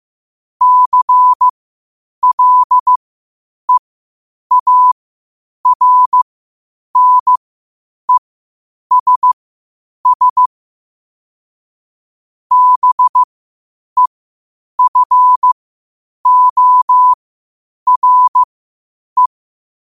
Morse code, heard as a keyed tone, about 15 wpm, spaced as for 8 wpm, 1000 Hz. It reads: CLEARNESS BEFORE
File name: qotdfiles / 8wpm